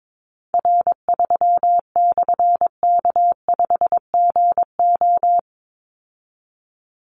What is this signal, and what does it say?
Morse code, heard as a keyed tone, about 22 wpm, 700 Hz. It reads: R3/K5GO
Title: Call Signs Repeated 3x - 22wpm